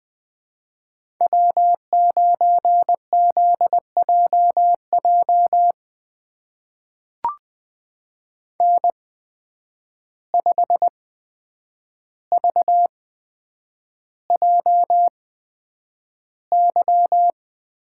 Morse code, heard as a keyed tone, about 20 wpm, spaced as for 9 wpm, 700 Hz. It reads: W9ZJJ N5VJY
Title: USA Call Signs - 5 Characters - 8x Character Spacing - 20wpm